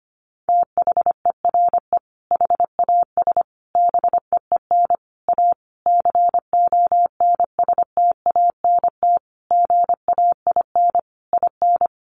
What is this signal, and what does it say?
Morse code, heard as a keyed tone, about 25 wpm, 700 Hz. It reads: THERE HAS BEEN A CONSTANT GAIN IN
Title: Sentences from Top 1500 Words 25wpm